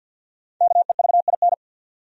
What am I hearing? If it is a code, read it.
Morse code, heard as a keyed tone, about 53 wpm, 700 Hz. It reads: KEVIN
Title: USA Common Names - 50wpm